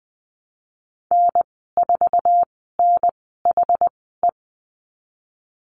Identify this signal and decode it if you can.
Morse code, heard as a keyed tone, about 20 wpm, spaced as for 15 wpm, 700 Hz. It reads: N4NHE
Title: 069 Lesson 12g - Letter d - Call Signs 20wpm